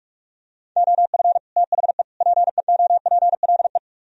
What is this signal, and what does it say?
Morse code, heard as a keyed tone, about 44 wpm, 700 Hz. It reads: OF THE PEOPLE